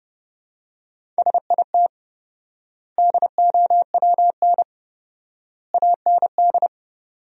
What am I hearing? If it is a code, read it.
Morse code, heard as a keyed tone, about 32 wpm, 700 Hz. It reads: SIT DOWN AND
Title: 4x Word-Spacing - Sentences from Top 500 Words - 30wpm